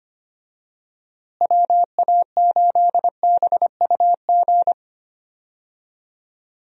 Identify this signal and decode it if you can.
Morse code, heard as a keyed tone, about 25 wpm, 700 Hz. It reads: WA8BUG
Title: USA Call Signs - 6 Characters - Repeated 2x - 25wpm